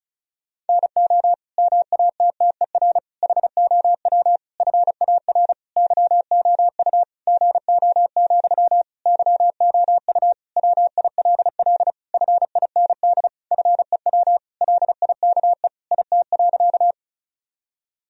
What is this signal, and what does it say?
Morse code, heard as a keyed tone, about 36 wpm, 700 Hz. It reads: NO MATTER HOW FAR YOU GO, YOU WILL FIND FEW LIKE IT.
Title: Sentences from Top 1000 Words - Review Part 1 - 35wpm